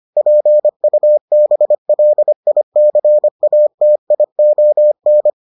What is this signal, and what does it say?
Morse code, heard as a keyed tone, about 25 wpm, 600 Hz. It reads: PUBLICATION